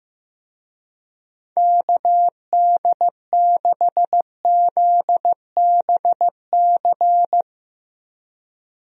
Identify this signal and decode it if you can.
Morse code, heard as a keyed tone, about 15 wpm, 700 Hz. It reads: KD6ZBC
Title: USA Call Signs - 6 Characters - Repeated 3x - 15wpm